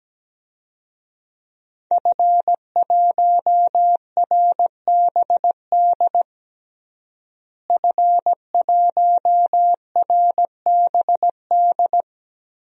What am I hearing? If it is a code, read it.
Morse code, heard as a keyed tone, about 17 wpm, 700 Hz. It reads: F1RBD F1RBD
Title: DXCC-100 Call Signs - 5 Characters - Repeated 2x - 17wpm